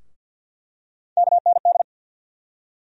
Morse code, heard as a keyed tone, about 52 wpm, 700 Hz. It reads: KND
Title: USA Call Sign Postfixes - 3 Letters 50wpm